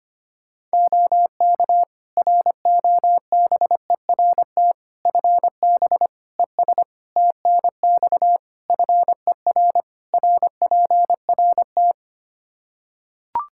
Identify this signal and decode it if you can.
Morse code, heard as a keyed tone, about 25 wpm, 700 Hz. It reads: OK ROBERT FB ES TNX FER RPRT
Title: K1USN SST QSO Practice 25wpm